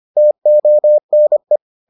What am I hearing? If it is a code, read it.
Morse code, heard as a keyed tone, about 25 wpm, 600 Hz. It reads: TONE